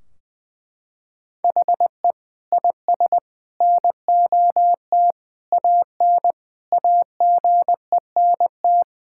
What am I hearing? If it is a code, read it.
Morse code, heard as a keyed tone, about 20 wpm, 700 Hz. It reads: HE IS NOT AN AGENT
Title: Sentences from Top 1600 Words 20wpm